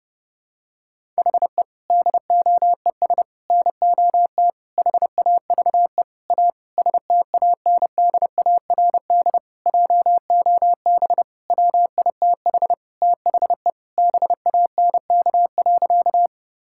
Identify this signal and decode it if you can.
Morse code, heard as a keyed tone, about 30 wpm, 700 Hz. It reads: HE DOES NOT HAVE A STANDARD JOB WITH THE BANK.
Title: Sentences from Top 1000 Words - Review Part 6 - 30wpm